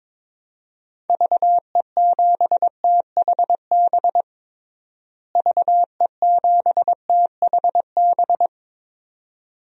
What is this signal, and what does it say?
Morse code, heard as a keyed tone, about 22 wpm, 700 Hz. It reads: VE7THB VE7THB
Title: DXCC-100 Call Signs - 6 Characters - Repeated 2x - 22wpm